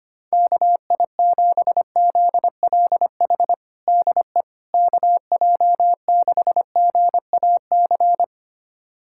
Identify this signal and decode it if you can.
Morse code, heard as a keyed tone, about 25 wpm, 700 Hz. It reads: KI7ZLH DE KJ6GAC
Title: USA Callsign DE Callsign - 6 Characters - 25wpm